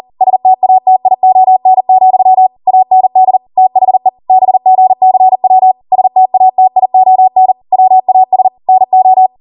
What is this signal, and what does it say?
Morse code, heard as a keyed tone, about 40 wpm, 750 Hz. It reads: STATION, AND THE BGCW STATION WAS NO